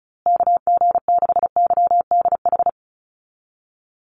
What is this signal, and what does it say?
Morse code, heard as a keyed tone, about 34 wpm, 700 Hz. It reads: KG6YDH
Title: USA Call Signs Repeated 3x 35wpm